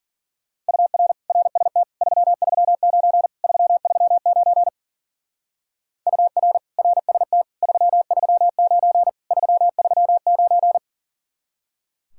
Morse code, heard as a keyed tone, about 48 wpm, 700 Hz. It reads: UR RST 339 339 UR RST 339 339
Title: Speed-Racing Common QSO Phrases Encore - 40wpm